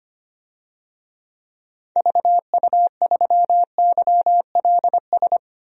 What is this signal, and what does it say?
Morse code, heard as a keyed tone, about 25 wpm, 700 Hz. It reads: VU3YLS
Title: DXCC-100 Call Signs - 6 Characters - 25wpm